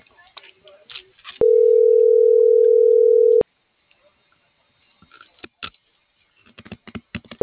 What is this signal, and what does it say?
Ringback tone (North America), 440 + 480 Hz.